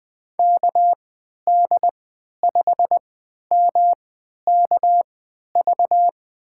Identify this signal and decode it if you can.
Morse code, heard as a keyed tone, about 20 wpm, spaced as for 11 wpm, 700 Hz. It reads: KD5MKV
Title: USA Call Signs - 3x Character Spacing - 20wpm